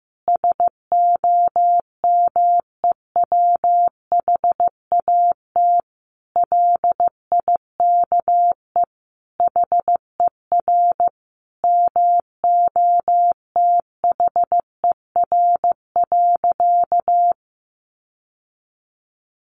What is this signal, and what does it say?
Morse code, heard as a keyed tone, about 15 wpm, 700 Hz. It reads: SOMEWHAT LIKE HER MOTHER.
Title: Sentences from Top 1300 Words 15wpm